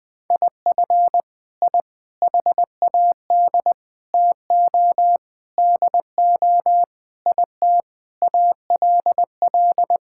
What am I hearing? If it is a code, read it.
Morse code, heard as a keyed tone, about 20 wpm, 700 Hz. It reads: IF I HAD TO DO IT ALL
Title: Sets of 9 Words - 20wpm